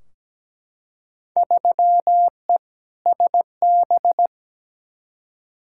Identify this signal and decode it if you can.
Morse code, heard as a keyed tone, about 17 wpm, 700 Hz. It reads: 3E SB
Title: Field Day Exchanges Encore - 17wpm